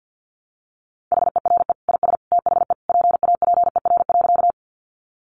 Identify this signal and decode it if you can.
Morse code, heard as a keyed tone, about 50 wpm, 700 Hz. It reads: HERE IS THE PAPER.
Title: Sentences from Top 300 Words 50wpm